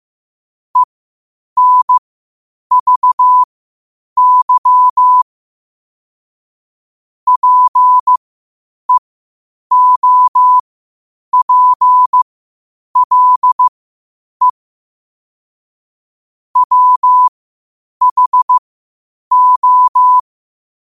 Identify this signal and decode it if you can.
Morse code, heard as a keyed tone, about 15 wpm, spaced as for 8 wpm, 1000 Hz. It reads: ENVY PEOPLE WHO
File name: qotdfiles / 8wpm